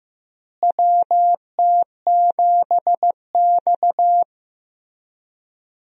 Morse code, heard as a keyed tone, about 16 wpm, 700 Hz. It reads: WT7X
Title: USA Call Signs - 4 Characters - Repeated 3x - 15wpm